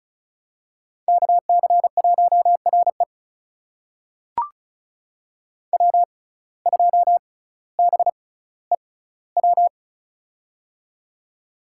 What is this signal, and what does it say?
Morse code, heard as a keyed tone, about 36 wpm, spaced as for 18 wpm, 700 Hz. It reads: KC1RE W2BEW
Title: USA Call Signs - 5 Characters - 6x Character Spacing - 35wpm